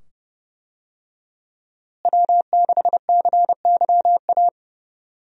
Morse code, heard as a keyed tone, about 29 wpm, 700 Hz. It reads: W6CYA
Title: USA Call Signs - 5 Characters - Repeated 2x - 30wpm